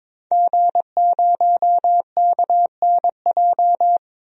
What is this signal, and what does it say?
Morse code, heard as a keyed tone, about 21 wpm, 700 Hz. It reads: G0KNJ
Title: DXCC-100 Call Signs - 5 Characters - 22wpm